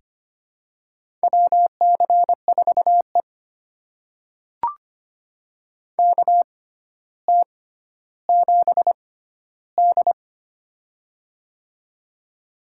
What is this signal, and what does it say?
Morse code, heard as a keyed tone, about 25 wpm, spaced as for 13 wpm, 700 Hz. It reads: WC4E KT7D
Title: USA Call Signs - 4 Characters - 6x Character Spacing - 25wpm